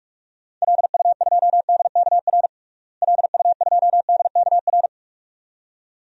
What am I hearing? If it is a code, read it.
Morse code, heard as a keyed tone, about 41 wpm, 700 Hz. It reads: LU2DKR LU2DKR
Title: DXCC-100 Call Signs - 6 Characters - Repeated 2x - 45wpm